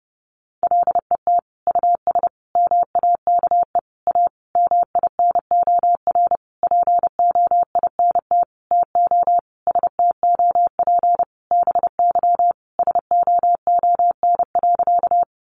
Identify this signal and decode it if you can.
Morse code, heard as a keyed tone, about 30 wpm, 700 Hz. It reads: LET US MAKE A MINOR POINT TO STOP BY SOON.
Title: Sentences from Top 1500 Words 30wpm